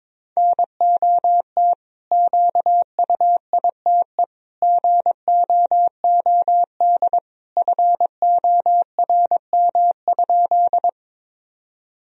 Morse code, heard as a keyed tone, about 22 wpm, 700 Hz. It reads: NOT QUITE GOOD FORM?
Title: Sentences from Top 1000 Words - Review Part 1 - 22wpm